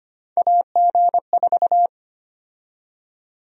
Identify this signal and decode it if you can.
Morse code, heard as a keyed tone, about 25 wpm, 700 Hz. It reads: AG4
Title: USA Call Sign Prefixes - 25wpm